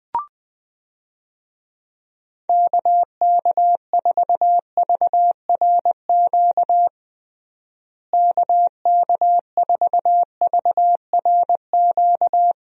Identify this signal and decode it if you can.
Morse code, heard as a keyed tone, about 21 wpm, 700 Hz. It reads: KK4VRQ KK4VRQ
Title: USA Call Signs - 6 Characters - Repeated 3x - 20wpm